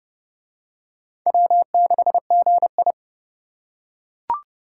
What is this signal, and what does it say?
Morse code, heard as a keyed tone, about 25 wpm, 700 Hz. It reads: W6GI